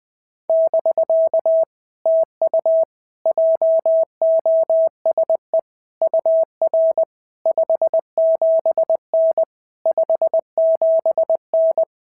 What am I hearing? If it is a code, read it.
Morse code, heard as a keyed tone, about 20 wpm, 650 Hz. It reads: <BK> TU JOSE UR 57N 57N
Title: POTA QSO Practice - 20wpm